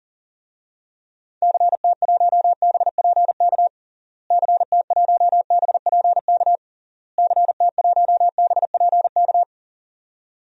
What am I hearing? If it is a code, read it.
Morse code, heard as a keyed tone, about 37 wpm, 700 Hz. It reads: CT1BPK CT1BPK CT1BPK
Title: DXCC-100 Call Signs - 6 Characters - Repeated 3x - 40wpm